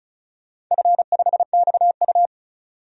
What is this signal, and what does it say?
Morse code, heard as a keyed tone, about 36 wpm, 700 Hz. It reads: F5XU
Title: DXCC-100 Call Signs - 4 Characters - 35wpm